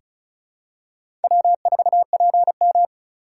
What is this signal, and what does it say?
Morse code, heard as a keyed tone, about 36 wpm, 700 Hz. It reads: W4PM
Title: USA Call Signs - 4 Characters - 3x Character Spacing - 35wpm